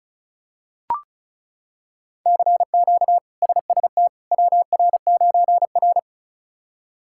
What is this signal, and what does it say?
Morse code, heard as a keyed tone, about 34 wpm, 700 Hz. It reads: CQ SST WR9R
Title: K1USN SST QSO Practice - 35wpm